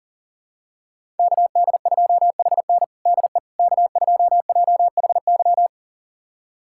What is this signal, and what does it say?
Morse code, heard as a keyed tone, about 37 wpm, 700 Hz. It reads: KD2HN DE K2JHY
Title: USA Callsign DE Callsign - 5 Characters - 40wpm